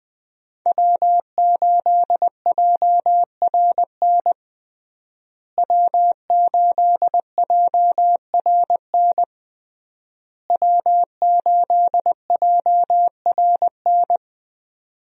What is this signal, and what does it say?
Morse code, heard as a keyed tone, about 20 wpm, 700 Hz. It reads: W8JRN W8JRN W8JRN
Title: USA Call Signs - 5 Characters - Repeated 3x - 20wpm